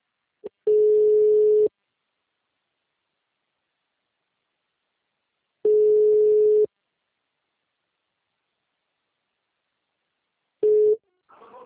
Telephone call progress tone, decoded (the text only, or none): ringback tone